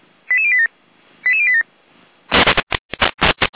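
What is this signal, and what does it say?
Five-tone selective call (ZVEI), heard as quick runs of five tones, tones of about 70 ms. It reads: 88087 88087
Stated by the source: ZVEI five-tone selective call received with an SDR.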